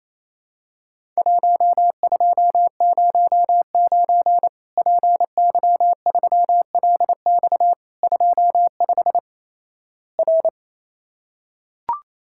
Morse code, heard as a keyed tone, about 28 wpm, 700 Hz. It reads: 1209 PY3LX 25 R
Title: Worked All Europe DX Contest - QTC Exchange - 28wpm